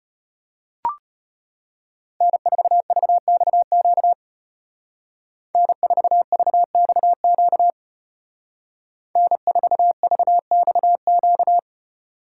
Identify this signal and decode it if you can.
Morse code, heard as a keyed tone, about 32 wpm, 700 Hz. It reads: N4VXQ N4VXQ N4VXQ